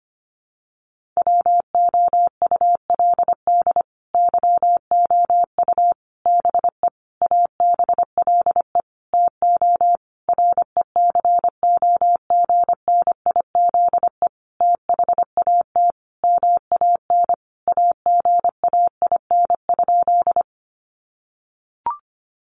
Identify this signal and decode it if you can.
Morse code, heard as a keyed tone, about 25 wpm, 700 Hz. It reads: WOULD YOU BE ABLE TO RECOGNIZE THAT MAN AGAIN?
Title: Sentences from Top 1400 Words 25wpm